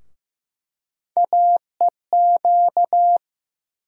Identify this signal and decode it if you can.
Morse code, heard as a keyed tone, about 15 wpm, 700 Hz. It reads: AEQ
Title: USA Call Sign Postfixes - 3 Letters 15wpm